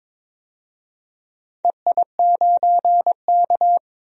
Morse code, heard as a keyed tone, about 21 wpm, 700 Hz. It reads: EI9K
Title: DXCC-100 Call Signs - 4 Characters - 22wpm